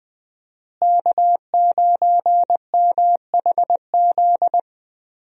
Morse code, heard as a keyed tone, about 20 wpm, 700 Hz. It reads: K9MHZ
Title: USA Call Signs - 5 Characters - Repeated 2x - 20wpm